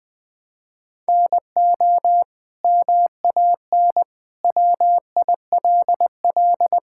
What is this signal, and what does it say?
Morse code, heard as a keyed tone, about 20 wpm, 700 Hz. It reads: NO MAN WILL
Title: Sentences from Top 1000 Words - Review Part 6 - 20wpm